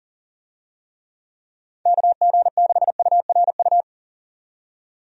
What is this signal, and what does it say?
Morse code, heard as a keyed tone, about 40 wpm, 700 Hz. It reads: KG6URU
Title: USA Call Signs - 6 Characters - 40wpm